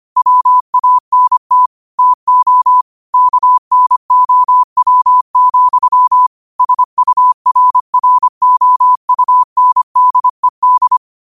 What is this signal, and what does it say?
Morse code, heard as a keyed tone, about 25 wpm, 1000 Hz. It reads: WANT TO KNOW, SURROUNDED